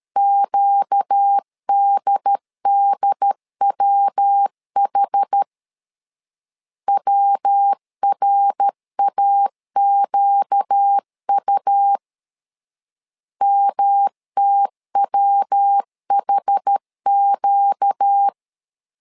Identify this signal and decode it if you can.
Morse code, heard as a keyed tone, about 13 wpm, 800 Hz. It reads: QDDWH WRAQU MTWHQ
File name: LEZIONI / 4 Q 12wpm 1833